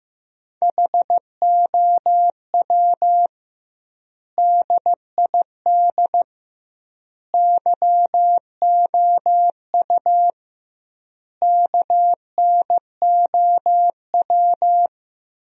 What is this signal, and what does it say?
Morse code, heard as a keyed tone, about 15 wpm, 700 Hz. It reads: HOW DID YOU KNOW
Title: 2x Word-Spacing - Sentences from Top 100 Words - 15wpm